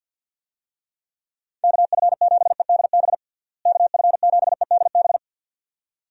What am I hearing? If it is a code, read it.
Morse code, heard as a keyed tone, about 51 wpm, 700 Hz. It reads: KF7EDB KF7EDB
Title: USA Call Signs - 6 Characters - Repeated 2x - 50wpm